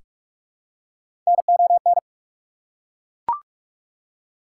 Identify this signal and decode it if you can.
Morse code, heard as a keyed tone, about 43 wpm, 690 Hz. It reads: NON
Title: Prefixes and Suffixes 45wpm